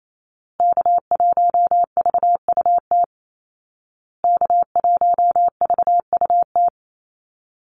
Morse code, heard as a keyed tone, about 28 wpm, 700 Hz. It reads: K1VUT K1VUT
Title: Call Signs Repeated 2x - 28wpm